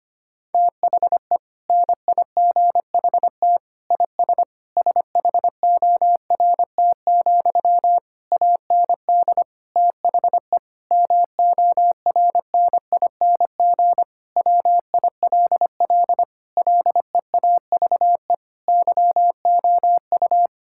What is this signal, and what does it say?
Morse code, heard as a keyed tone, about 25 wpm, 700 Hz. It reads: THE NIGHT IS SHORT, AND THE MORNING WILL LEAVE YOU
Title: Sentences from Top 1600 Words 25wpm